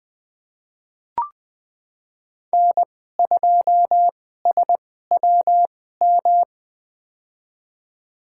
Morse code, heard as a keyed tone, about 20 wpm, spaced as for 15 wpm, 700 Hz. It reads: N2SWM
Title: USA Call Signs - 2x Character Spacing - 20wpm